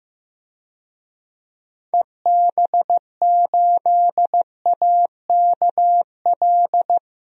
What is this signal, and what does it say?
Morse code, heard as a keyed tone, about 15 wpm, 700 Hz. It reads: EB8AKL
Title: DXCC-100 Call Signs - 6 Characters - 15wpm